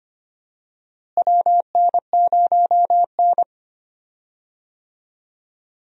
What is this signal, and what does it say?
Morse code, heard as a keyed tone, about 25 wpm, 700 Hz. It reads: WN0N
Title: USA Call Signs - 4 Characters - Repeated 2x - 25wpm